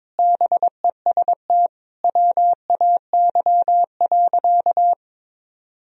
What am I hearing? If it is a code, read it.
Morse code, heard as a keyed tone, about 22 wpm, 700 Hz. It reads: BEST WAY.
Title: Sentences from Top 400 Words and Abbreviations 22wpm